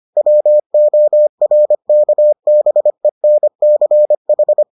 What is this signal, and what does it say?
Morse code, heard as a keyed tone, about 25 wpm, 600 Hz. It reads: WORKBENCH